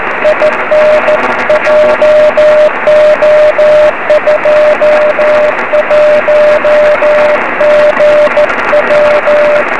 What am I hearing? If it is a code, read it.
Morse code, heard as a keyed tone, about 15 wpm, 590 Hz, with noise. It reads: IN JO21GW
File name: 2108z pa3awj 23cm